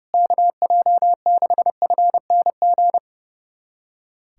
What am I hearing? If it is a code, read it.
Morse code, heard as a keyed tone, about 30 wpm, 700 Hz. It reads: KJ6FNG
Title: USA Callsign DE Callsign - 6 Characters - 30wpm